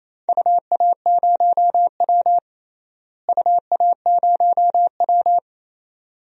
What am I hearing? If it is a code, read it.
Morse code, heard as a keyed tone, about 28 wpm, 700 Hz. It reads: UA0W UA0W
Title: DXCC-100 Call Signs - 4 Characters - Repeated 3x - 28wpm